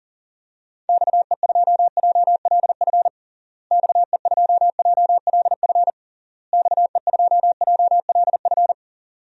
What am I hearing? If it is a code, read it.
Morse code, heard as a keyed tone, about 40 wpm, 700 Hz. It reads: XE2JLF XE2JLF XE2JLF
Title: DXCC-100 Call Signs - 6 Characters - Repeated 3x - 40wpm